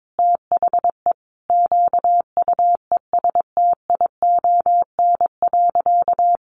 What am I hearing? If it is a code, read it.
Morse code, heard as a keyed tone, about 22 wpm, 700 Hz. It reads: THE QUESTION.
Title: Sentences from Top 1000 Words 22wpm